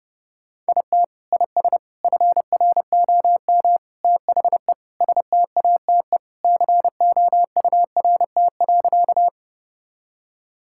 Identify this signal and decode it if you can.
Morse code, heard as a keyed tone, about 30 wpm, 700 Hz. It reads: IT IS FROM THE STATE COURT.
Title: Sentences from Top 1600 Words 30wpm